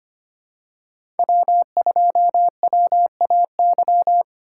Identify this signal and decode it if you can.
Morse code, heard as a keyed tone, about 25 wpm, 700 Hz. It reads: W2WAY
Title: USA Call Signs - 5 Characters - 2x Character Spacing - 25wpm